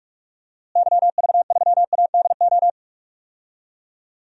Morse code, heard as a keyed tone, about 46 wpm, 700 Hz. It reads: YV3ADO
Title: DXCC-100 Call Signs - 6 Characters - Repeated 3x - 45wpm